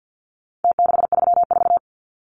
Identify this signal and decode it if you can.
Morse code, heard as a keyed tone, about 49 wpm, 700 Hz. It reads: T634
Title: Numbers - Contest Style - 4-Digit 50wpm